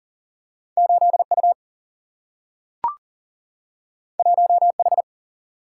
Morse code, heard as a keyed tone, about 40 wpm, 700 Hz. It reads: 8U 1H
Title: Single-Letter Single-Number 40wpm